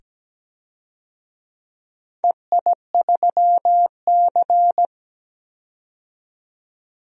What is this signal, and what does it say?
Morse code, heard as a keyed tone, about 17 wpm, 700 Hz. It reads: EI3C
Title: DXCC-100 Call Signs - 4 Characters - 17wpm